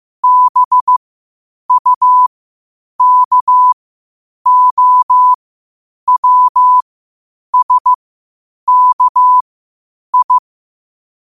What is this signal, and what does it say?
Morse code, heard as a keyed tone, about 15 wpm, spaced as for 8 wpm, 1000 Hz. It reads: BUKOWSKI
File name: qotdfiles / 8wpm